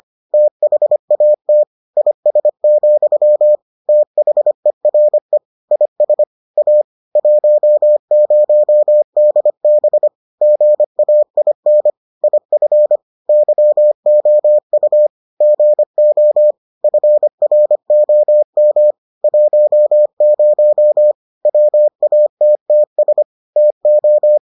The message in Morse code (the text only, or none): THAT IS, THERE IS A 10DB GAIN IF YOU GO FROM 10 WATTS TO